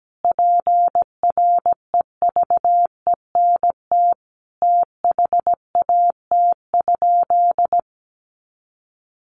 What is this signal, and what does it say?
Morse code, heard as a keyed tone, about 17 wpm, 700 Hz. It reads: PREVENT THAT?
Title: Sentences from Top 1100 Words 17wpm